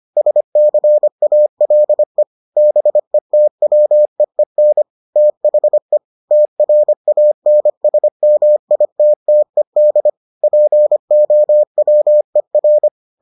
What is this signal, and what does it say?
Morse code, heard as a keyed tone, about 25 wpm, 600 Hz. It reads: SCALE BETWEEN THE TRANSMITTED POWER